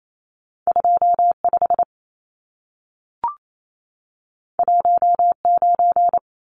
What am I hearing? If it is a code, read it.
Morse code, heard as a keyed tone, about 28 wpm, 700 Hz. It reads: 25 19
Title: Numbers - 2-Digit 28wpm